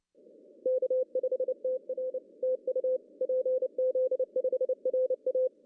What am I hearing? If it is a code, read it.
Morse code, heard as a keyed tone, about 29 wpm, 500 Hz. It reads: K5TR TU PZ5RA
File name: CW / cqww-cw-k5tr